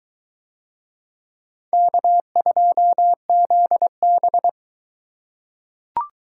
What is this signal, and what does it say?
Morse code, heard as a keyed tone, about 22 wpm, 700 Hz. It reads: K2ZB